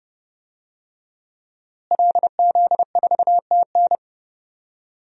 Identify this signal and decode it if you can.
Morse code, heard as a keyed tone, about 30 wpm, 700 Hz. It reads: LZ4TN
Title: DXCC-100 Call Signs - 5 Characters - Repeated 3x - 30wpm